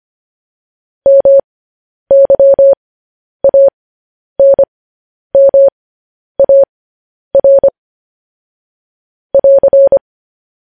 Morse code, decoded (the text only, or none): MYANMAR <AR>